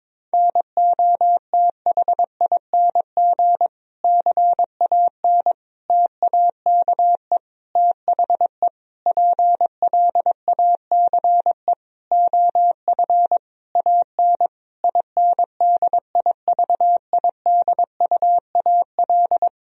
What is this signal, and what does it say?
Morse code, heard as a keyed tone, about 22 wpm, 700 Hz. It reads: NOTHING CAN TAKE THE PLACE OF AN INDIVIDUAL
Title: Sentences from Top 1000 Words - Review Part 4 - 22wpm